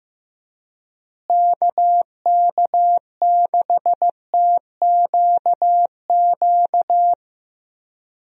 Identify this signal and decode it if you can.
Morse code, heard as a keyed tone, about 15 wpm, 700 Hz. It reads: KK6TQQ
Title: USA Call Signs - 6 Characters - Repeated 3x - 15wpm